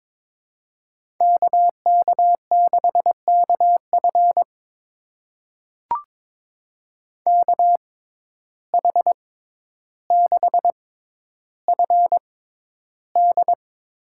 Morse code, heard as a keyed tone, about 22 wpm, spaced as for 11 wpm, 700 Hz. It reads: KK6KF KH6FD
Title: USA Call Signs - 6x Character Spacing - 22wpm